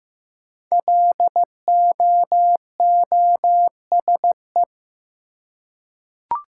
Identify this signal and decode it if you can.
Morse code, heard as a keyed tone, about 15 wpm, 700 Hz. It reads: LOOSE
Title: Five Letter Words - 15wpm